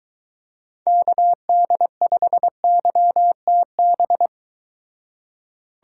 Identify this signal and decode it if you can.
Morse code, heard as a keyed tone, about 23 wpm, 700 Hz. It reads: KD5YTB